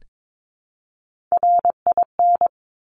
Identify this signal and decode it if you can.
Morse code, heard as a keyed tone, about 22 wpm, 700 Hz. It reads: RIN
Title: Top Three Letter Combinations in Words 22wpm